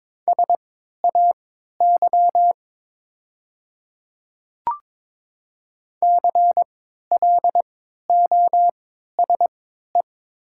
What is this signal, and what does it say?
Morse code, heard as a keyed tone, about 22 wpm, spaced as for 12 wpm, 700 Hz. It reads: SAY CLOSE
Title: 3x Character Spacing - Top 300 Words - 22wpm